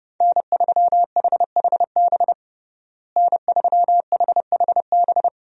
Morse code, heard as a keyed tone, about 30 wpm, 700 Hz. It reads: N3HHB N3HHB
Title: USA Call Signs - 5 Characters - Repeated 2x - 30wpm